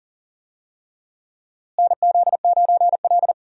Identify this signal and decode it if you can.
Morse code, heard as a keyed tone, about 40 wpm, 700 Hz. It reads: NZ9L
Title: USA Call Signs - 4 Characters - Repeated 3x - 40wpm